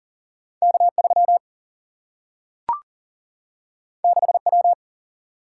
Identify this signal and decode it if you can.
Morse code, heard as a keyed tone, about 39 wpm, 700 Hz. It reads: K 3 6 W
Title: Single-Letter Single-Number 40wpm